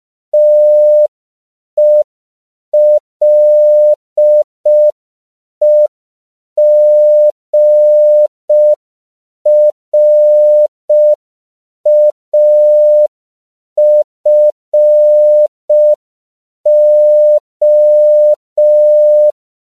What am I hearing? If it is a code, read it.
Morse code, heard as a keyed tone, about 5 wpm, 600 Hz. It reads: TELEGRAFO